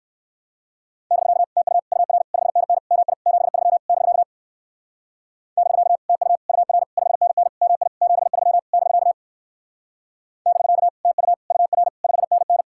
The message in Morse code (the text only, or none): <BK> TU UR 5NN MI 73 <BK> <BK> TU UR 5NN MI 73 <BK> <BK> TU UR 5NN